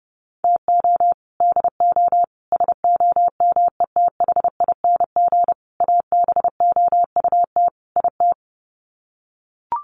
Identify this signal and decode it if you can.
Morse code, heard as a keyed tone, about 30 wpm, 700 Hz. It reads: TO DO SOMETHING ABOUT IT
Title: Sets of 5 Words 30wpm